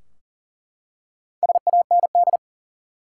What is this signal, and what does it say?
Morse code, heard as a keyed tone, about 40 wpm, 700 Hz. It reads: SAND
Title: Four Letter Words - 40wpm